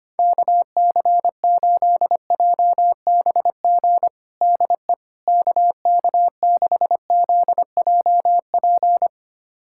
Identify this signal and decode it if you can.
Morse code, heard as a keyed tone, about 25 wpm, 700 Hz. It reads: KC8JBG DE KK6ZJP
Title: USA Callsign DE Callsign - 6 Characters - 25wpm